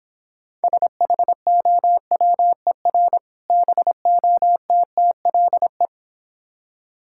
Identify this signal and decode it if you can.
Morse code, heard as a keyed tone, about 26 wpm, 700 Hz. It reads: SHOWER BOTTLE